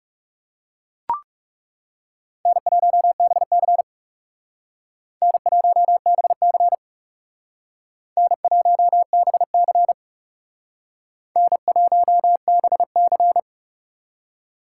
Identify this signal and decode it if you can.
Morse code, heard as a keyed tone, about 39 wpm, 700 Hz. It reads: N1BC N1BC N1BC N1BC